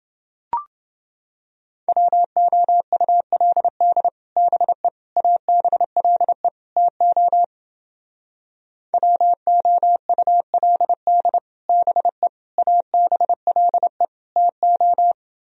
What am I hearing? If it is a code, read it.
Morse code, heard as a keyed tone, about 30 wpm, 700 Hz. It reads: WOULD BE ABLE TO WOULD BE ABLE TO